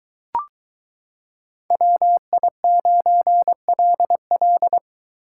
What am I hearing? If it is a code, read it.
Morse code, heard as a keyed tone, about 23 wpm, 700 Hz. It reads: WI9LL